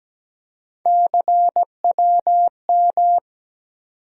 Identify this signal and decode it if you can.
Morse code, heard as a keyed tone, about 17 wpm, 700 Hz. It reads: CWM
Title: USA Call Sign Postfixes - 3 Letters 17wpm